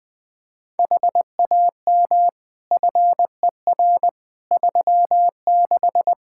Morse code, heard as a keyed tone, about 21 wpm, 700 Hz. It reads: HAM FER 36
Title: K1USN SST QSO Practice 20wpm